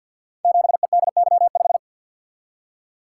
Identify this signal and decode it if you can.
Morse code, heard as a keyed tone, about 52 wpm, 700 Hz. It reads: 7EDY5
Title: Code Groups - 5 Letter:Numbers - 50wpm